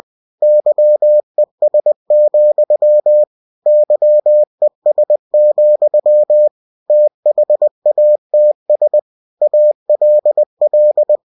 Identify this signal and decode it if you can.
Morse code, heard as a keyed tone, about 20 wpm, 600 Hz. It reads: YES, YES, THATS ALL
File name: cw-books / winnie-the-pooh / winnie-20wpm-page-0065